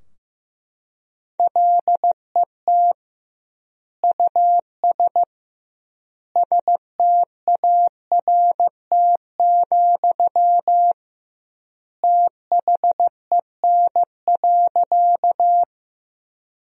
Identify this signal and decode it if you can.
Morse code, heard as a keyed tone, about 15 wpm, 700 Hz. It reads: LET US START, THEN.
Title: 2x Word-Spacing - Sentences from Top 300 Words - 15wpm